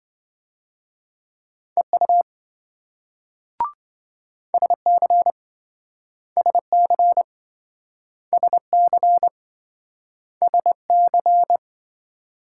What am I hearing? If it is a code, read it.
Morse code, heard as a keyed tone, about 27 wpm, 700 Hz. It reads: EU SC SC SC SC